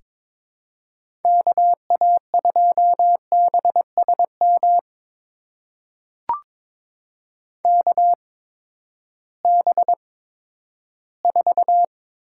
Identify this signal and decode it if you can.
Morse code, heard as a keyed tone, about 22 wpm, spaced as for 12 wpm, 700 Hz. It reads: KA2BSM KB4
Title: USA Call Signs - 6 Characters - 8x Character Spacing - 22wpm